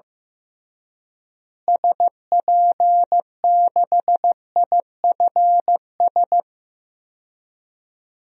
Morse code, heard as a keyed tone, about 15 wpm, 700 Hz. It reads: SP6IFS
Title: DXCC-100 Call Signs - 6 Characters - Repeated 2x - 15wpm